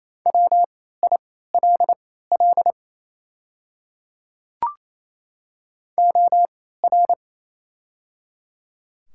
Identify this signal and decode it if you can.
Morse code, heard as a keyed tone, about 28 wpm, spaced as for 16 wpm, 700 Hz. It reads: WILL OR
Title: 3x Character Spacing - Top 100 Words - 28wpm